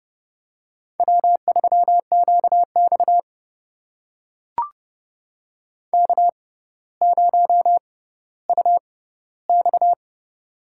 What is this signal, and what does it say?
Morse code, heard as a keyed tone, about 30 wpm, spaced as for 15 wpm, 700 Hz. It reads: W3QX K0UX
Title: USA Call Signs - 6x Character Spacing - 30wpm